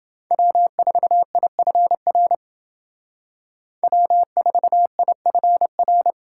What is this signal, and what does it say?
Morse code, heard as a keyed tone, about 28 wpm, 700 Hz. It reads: W4IFR W4IFR